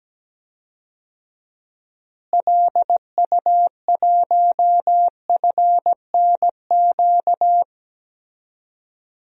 Morse code, heard as a keyed tone, about 17 wpm, 700 Hz. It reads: LU1FNQ
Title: DXCC-100 Call Signs - 6 Characters - Repeated 2x - 17wpm